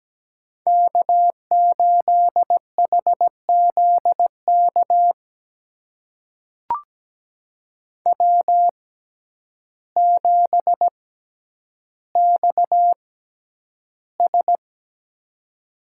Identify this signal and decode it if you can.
Morse code, heard as a keyed tone, about 17 wpm, spaced as for 9 wpm, 700 Hz. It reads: K8HZK W7XS
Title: USA Call Signs - 6x Character Spacing - 17wpm